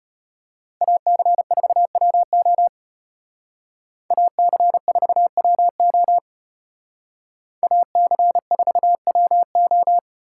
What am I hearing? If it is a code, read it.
Morse code, heard as a keyed tone, about 31 wpm, 700 Hz. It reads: AC4WO AC4WO AC4WO
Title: Speed Racing - USA 2x2 Call Signs 25wpm